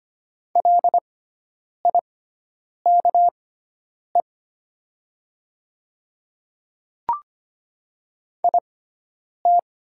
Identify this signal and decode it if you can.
Morse code, heard as a keyed tone, about 25 wpm, spaced as for 9 wpm, 700 Hz. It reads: LIKE IT
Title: 6x Character Spacing - Top 100 Words - 25wpm